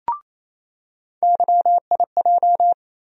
Morse code, heard as a keyed tone, about 28 wpm, 700 Hz. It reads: YIJ
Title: USA Call Sign Postfixes - 3 Letters 28wpm